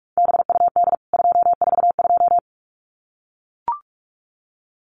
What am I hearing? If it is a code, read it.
Morse code, heard as a keyed tone, about 44 wpm, 700 Hz. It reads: BUD 242
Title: CWT Exchange 45wpm